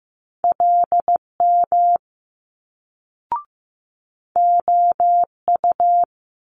Morse code, heard as a keyed tone, about 15 wpm, 700 Hz. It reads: LM OU
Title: Multi-letter Phonemes Quick Fire 15wpm